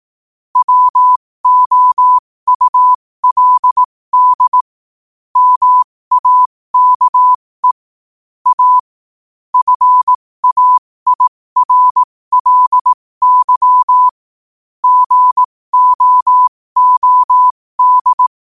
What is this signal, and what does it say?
Morse code, heard as a keyed tone, about 18 wpm, spaced as for 15 wpm, 1000 Hz. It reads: WOULD MAKE A FAIRLY GOOD
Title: 15 WPM morse code quotes for Thu, 29 Feb 2024